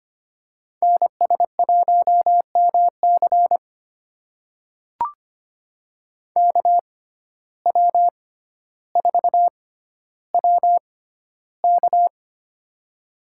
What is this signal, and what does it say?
Morse code, heard as a keyed tone, about 25 wpm, spaced as for 13 wpm, 700 Hz. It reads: NS1MC KW4WK
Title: USA Call Signs - 5 Characters - 6x Character Spacing - 25wpm